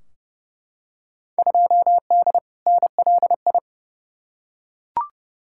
Morse code, heard as a keyed tone, about 30 wpm, 700 Hz. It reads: 2D NLI
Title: Field Day Exchanges Encore - 30wpm